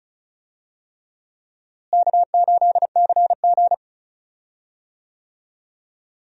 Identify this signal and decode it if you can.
Morse code, heard as a keyed tone, about 34 wpm, 700 Hz. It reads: K8CG
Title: USA Call Signs - 4 Characters - Repeated 3x - 35wpm